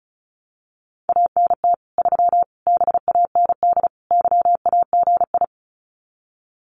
Morse code, heard as a keyed tone, about 34 wpm, 700 Hz. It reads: ANT 3 BAND YAGI
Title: Common QSO Phrases 35wpm